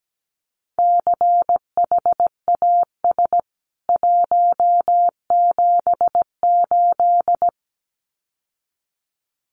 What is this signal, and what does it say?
Morse code, heard as a keyed tone, about 17 wpm, 700 Hz. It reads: CHAS 178
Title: CWT Exchange 17wpm